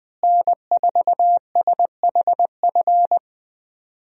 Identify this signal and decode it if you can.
Morse code, heard as a keyed tone, about 21 wpm, 700 Hz. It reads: N4SHF
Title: USA Call Signs - 5 Characters - Repeated 2x - 20wpm